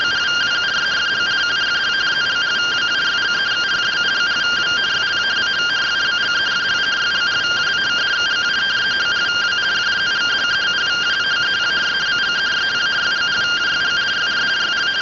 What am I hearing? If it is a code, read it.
Two-tone teleprinter signal, about 170 Hz shift.
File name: findings / CT 205 ax mips, iDose 3 0006.gen-14 rtty rx